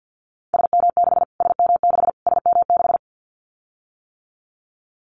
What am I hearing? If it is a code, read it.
Morse code, heard as a keyed tone, about 50 wpm, 700 Hz. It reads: SM6 SM6 SM6
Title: Callsign Prefixes for Top 25 DXCC - Repeated 3x - 50wpm